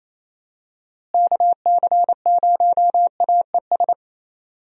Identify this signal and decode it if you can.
Morse code, heard as a keyed tone, about 28 wpm, 700 Hz. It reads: KC0AES
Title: USA Call Signs - 6 Characters - Repeated 3x - 28wpm